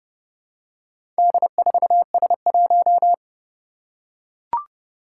Morse code, heard as a keyed tone, about 30 wpm, 700 Hz. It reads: D4S1
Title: Four Challenging Letter-Number Combinations 30wpm